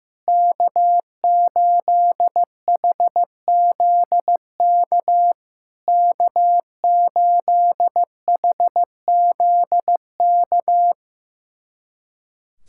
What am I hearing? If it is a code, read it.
Morse code, heard as a keyed tone, about 15 wpm, 690 Hz. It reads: K8HZK K8HZK
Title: SOTA QSO Practice 15wpm